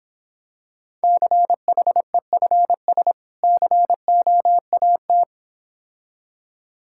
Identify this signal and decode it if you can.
Morse code, heard as a keyed tone, about 26 wpm, 700 Hz. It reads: CHEFS COAT